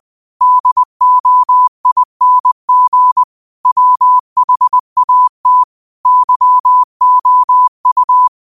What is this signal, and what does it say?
Morse code, heard as a keyed tone, about 20 wpm, 1000 Hz. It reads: DOING WHAT YOU
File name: qotdfiles / 20wpm